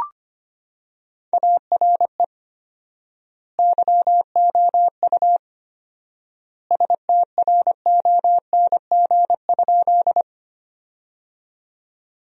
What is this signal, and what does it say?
Morse code, heard as a keyed tone, about 25 wpm, 700 Hz. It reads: ARE YOU STRONG?
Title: 4x Word-Spacing - Sentences from Top 500 Words - 25wpm